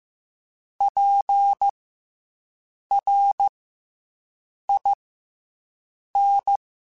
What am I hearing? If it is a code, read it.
Morse code, heard as a keyed tone, about 15 wpm, 800 Hz. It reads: P R I N
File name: Special INGs / Special INGs-mp3s / SlowToFaster ing @13